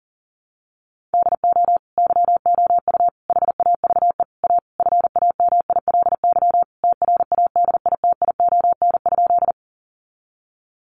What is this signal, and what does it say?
Morse code, heard as a keyed tone, about 37 wpm, 700 Hz. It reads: DO YOU HAVE A FAMILY TRADITION?
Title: Sentences from Top 1500 Words 40wpm